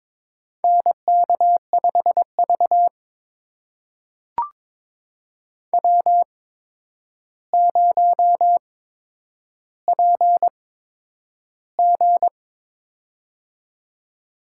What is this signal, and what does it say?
Morse code, heard as a keyed tone, about 22 wpm, spaced as for 9 wpm, 700 Hz. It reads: NK5V W0PG
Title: USA Call Signs - 4 Characters - 8x Character Spacing - 22wpm